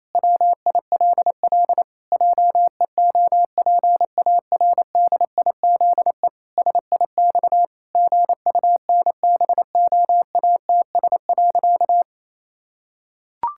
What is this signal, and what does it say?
Morse code, heard as a keyed tone, about 28 wpm, 700 Hz. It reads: WILL JEOPARDIZE SIX GUNBOATS.
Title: Pangram Challenge v2 - 28wpm